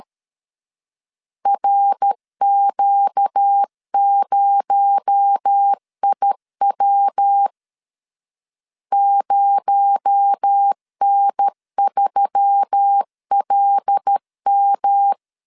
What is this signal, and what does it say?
Morse code, heard as a keyed tone, about 13 wpm, 800 Hz. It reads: RQ0IW 0N3LM
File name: LEZIONI / 6 PKX0 / 6 PKX0 12wpm 2231